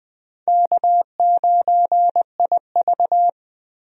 Morse code, heard as a keyed tone, about 20 wpm, 700 Hz. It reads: K9IV
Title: USA Call Signs - 4 Characters - Repeated 3x - 20wpm